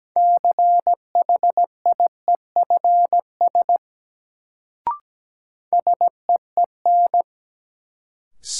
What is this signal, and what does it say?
Morse code, heard as a keyed tone, about 17 wpm, 700 Hz. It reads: CHIEFS SEEN
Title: Sets of 5 Words Encore - New Word Drill - 17wpm